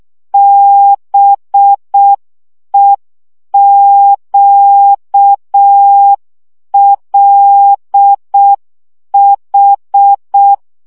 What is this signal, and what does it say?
Morse code, heard as a keyed tone, about 6 wpm, 800 Hz. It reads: BEQLH
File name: morze vizsga 2011-2 5